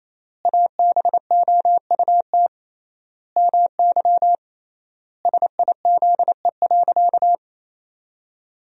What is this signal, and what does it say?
Morse code, heard as a keyed tone, about 28 wpm, 700 Hz. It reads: ABOUT MY SIZE.
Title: 3x Word-Spacing - Sentences from Top 400 Words - 28wpm